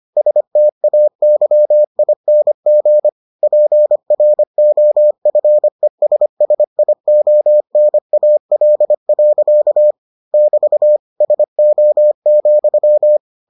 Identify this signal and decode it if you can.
Morse code, heard as a keyed tone, about 25 wpm, 600 Hz. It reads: STAYING PROFESSIONAL. <BT> SO,